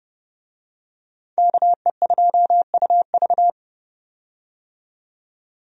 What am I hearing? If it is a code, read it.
Morse code, heard as a keyed tone, about 30 wpm, 700 Hz. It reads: KE2UV
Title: USA Call Signs - 5 Characters - 30wpm